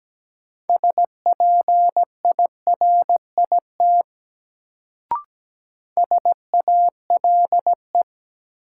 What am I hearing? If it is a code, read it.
Morse code, heard as a keyed tone, about 17 wpm, 700 Hz. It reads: SPIRIT SALE
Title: Sentences from Top 1400 Words - New Word Drill - 17wpm